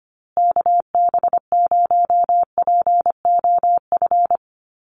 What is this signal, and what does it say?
Morse code, heard as a keyed tone, about 25 wpm, 700 Hz. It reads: KB0POF
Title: USA Call Signs Repeated 2x 25wpm